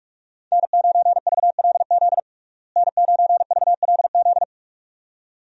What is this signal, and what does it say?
Morse code, heard as a keyed tone, about 45 wpm, 700 Hz. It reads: N9VLZ N9VLZ
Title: USA Call Signs - 5 Characters - Repeated 2x - 45wpm